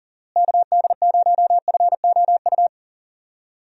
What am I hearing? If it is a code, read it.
Morse code, heard as a keyed tone, about 40 wpm, 700 Hz. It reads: KD0FOU
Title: USA Call Signs - 6 Characters - Repeated 2x - 40wpm